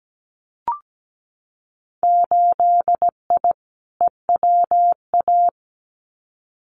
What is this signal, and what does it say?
Morse code, heard as a keyed tone, about 17 wpm, 700 Hz. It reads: 8I EWA
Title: Winter Field Day Exchanges Encore 17wpm